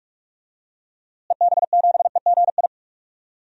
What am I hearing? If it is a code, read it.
Morse code, heard as a keyed tone, about 46 wpm, 700 Hz. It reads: EB7EGI
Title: DXCC-100 Call Signs - 6 Characters - Repeated 3x - 45wpm